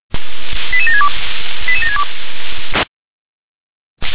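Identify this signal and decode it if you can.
Five-tone selective call (ZVEI), heard as quick runs of five tones, tones of about 80 ms. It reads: 88762 88762
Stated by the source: ZVEI five-tone call received by SDR.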